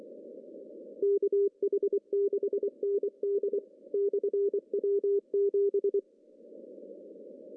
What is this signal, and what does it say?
Morse code, heard as a keyed tone, about 25 wpm, 400 Hz. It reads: KH6ND/W7
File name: CW / cqww-cw-k5tr